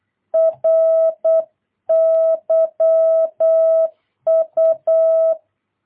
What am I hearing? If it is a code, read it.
Morse code, heard as a keyed tone, about 8 wpm, 660 Hz. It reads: RYU
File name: jh8cbh / 2022cw